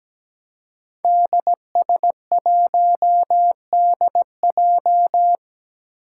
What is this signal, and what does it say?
Morse code, heard as a keyed tone, about 17 wpm, 700 Hz. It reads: DS1DJ
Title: Five Challenging Letter-Number Combinations 17wpm